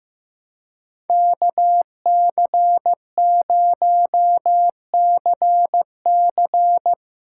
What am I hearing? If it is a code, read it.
Morse code, heard as a keyed tone, about 15 wpm, 700 Hz. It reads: KC0CC
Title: USA Call Signs - 5 Characters - 2x Character Spacing - 15wpm